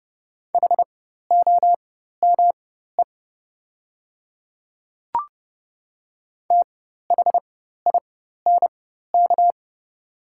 Morse code, heard as a keyed tone, about 30 wpm, spaced as for 14 wpm, 700 Hz. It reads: HOME THINK
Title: 4x Character Spacing - Top 200 Words - 30wpm